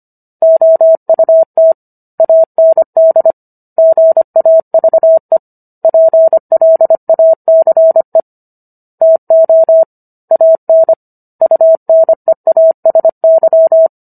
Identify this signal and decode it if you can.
Morse code, heard as a keyed tone, about 25 wpm, 650 Hz. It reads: OUT AND GAVE PLACE TO AN UNEASY